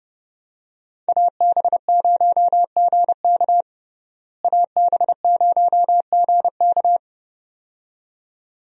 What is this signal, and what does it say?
Morse code, heard as a keyed tone, about 30 wpm, 700 Hz. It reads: AB0GK AB0GK
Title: USA Call Signs - 5 Characters - Repeated 2x - 30wpm